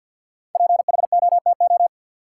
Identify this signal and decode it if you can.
Morse code, heard as a keyed tone, about 53 wpm, 700 Hz. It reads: PHOTO
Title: Sentences from Top 1200 Words - New Word Drill - 50wpm